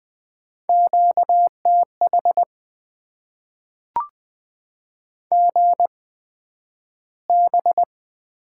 Morse code, heard as a keyed tone, about 20 wpm, spaced as for 11 wpm, 700 Hz. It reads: QTH GB
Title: Top 90 QSO Elements - 8x Character Spacing - 20wpm